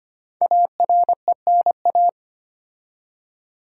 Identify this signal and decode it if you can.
Morse code, heard as a keyed tone, about 25 wpm, 700 Hz. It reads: ARENA
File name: 5 Letter Words - First 3 Letters Given - 25wpm